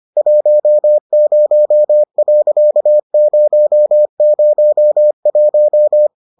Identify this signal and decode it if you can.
Morse code, heard as a keyed tone, about 25 wpm, 600 Hz. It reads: 10.001